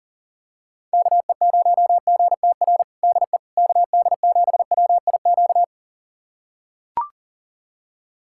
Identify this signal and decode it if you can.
Morse code, heard as a keyed tone, about 40 wpm, 700 Hz. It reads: KE0GTR DE KD7WIQ
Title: USA Callsign DE Callsign - 6 Characters - 40wpm